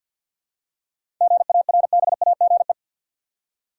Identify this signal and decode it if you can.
Morse code, heard as a keyed tone, about 50 wpm, 700 Hz. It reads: GARBAGE
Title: Seven Letter Words - 50wpm